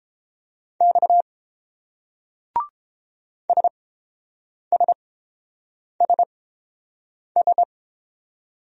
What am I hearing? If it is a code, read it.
Morse code, heard as a keyed tone, about 30 wpm, spaced as for 7 wpm, 700 Hz. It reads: X SSSS